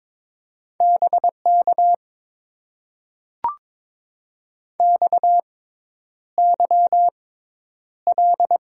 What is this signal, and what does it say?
Morse code, heard as a keyed tone, about 22 wpm, spaced as for 10 wpm, 700 Hz. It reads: BK XYL
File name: Top 90 QSO Elements - 6x Character Spacing - 22wpm